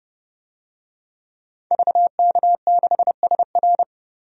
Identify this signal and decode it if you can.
Morse code, heard as a keyed tone, about 30 wpm, 700 Hz. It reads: VK6SR
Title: DXCC-100 Call Signs - 5 Characters - 30wpm